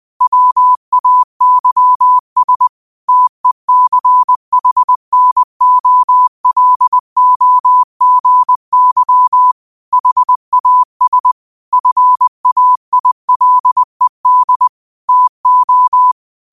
Morse code, heard as a keyed tone, about 20 wpm, 1000 Hz. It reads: WAYS TECHNOLOGY HAS FAILED TO
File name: qotdfiles / 20wpm